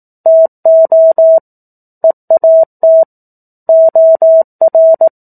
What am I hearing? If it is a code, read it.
Morse code, heard as a keyed tone, about 18 wpm, 650 Hz. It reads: TO EAT OR
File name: 18 WPM